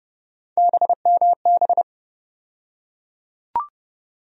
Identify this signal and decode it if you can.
Morse code, heard as a keyed tone, about 30 wpm, 700 Hz. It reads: BMB
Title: USA Call Sign Postfixes - 3 Letters 30wpm